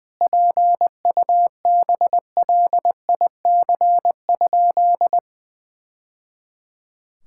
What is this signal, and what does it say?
Morse code, heard as a keyed tone, about 20 wpm, 700 Hz. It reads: PUBLIC?
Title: Sentences from Top 1000 Words - Review Part 5 - 20wpm